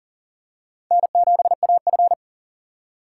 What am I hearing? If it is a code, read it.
Morse code, heard as a keyed tone, about 40 wpm, 700 Hz. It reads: N7AF
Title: USA Call Signs - 4 Characters - 2x Character Spacing - 40wpm